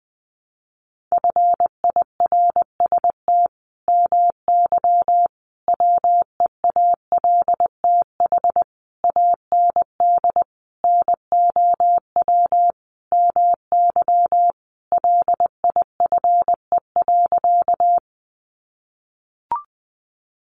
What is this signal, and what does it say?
Morse code, heard as a keyed tone, about 20 wpm, 700 Hz. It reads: FIRST MY WEALTH AND NOW MY LIFE.
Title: Sentences from Top 1200 Words 20wpm